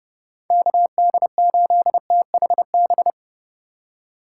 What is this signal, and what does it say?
Morse code, heard as a keyed tone, about 29 wpm, 700 Hz. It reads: KD8THB
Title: USA Call Signs - 6 Characters - 2x Character Spacing - 30wpm